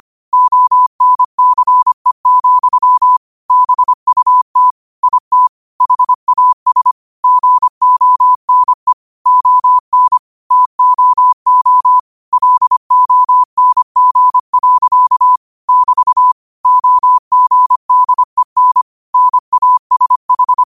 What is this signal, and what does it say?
Morse code, heard as a keyed tone, about 25 wpm, 1000 Hz. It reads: ONCE, BUT IT HAS GONE ON TOO LONG. <BT> OGDEN NASH
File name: qotdfiles / 25wpm